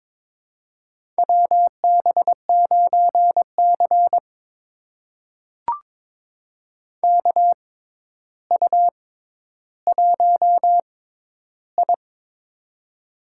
Morse code, heard as a keyed tone, about 22 wpm, spaced as for 11 wpm, 700 Hz. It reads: WB9C KU1I
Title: USA Call Signs - 6x Character Spacing - 22wpm